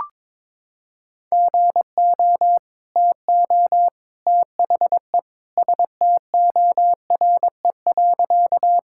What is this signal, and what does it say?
Morse code, heard as a keyed tone, about 22 wpm, 700 Hz. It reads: GO TO THE STORE.
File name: practice / 22wpm / Sentences with 2-Word Phrases - 22wpm